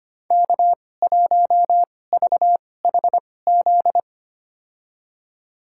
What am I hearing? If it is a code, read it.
Morse code, heard as a keyed tone, about 25 wpm, spaced as for 18 wpm, 700 Hz. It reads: K1VHZ
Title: USA Call Signs - 2x Character Spacing - 25wpm